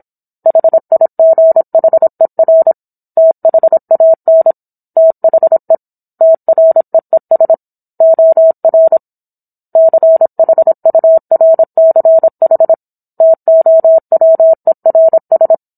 Morse code, heard as a keyed tone, about 27 wpm, 650 Hz. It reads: HIGHER THAN THE TREES OR CHURCH TOWERS